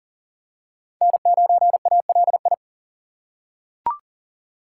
Morse code, heard as a keyed tone, about 40 wpm, 700 Hz. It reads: N9ALI
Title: USA Call Signs - 3x Character Spacing - 40wpm